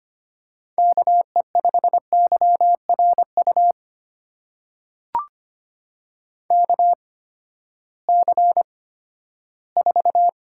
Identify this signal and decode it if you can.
Morse code, heard as a keyed tone, about 25 wpm, spaced as for 14 wpm, 700 Hz. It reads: KE5YRU KC4
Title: USA Call Signs - 6 Characters - 8x Character Spacing - 25wpm